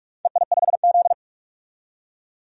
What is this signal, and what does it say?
Morse code, heard as a keyed tone, about 46 wpm, 700 Hz. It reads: EI5Z
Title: DXCC-100 Call Signs - 4 Characters - 45wpm